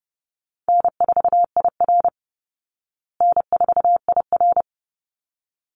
Morse code, heard as a keyed tone, about 30 wpm, 700 Hz. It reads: N4IR N4IR
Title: Call Signs Repeated 3x - 30wpm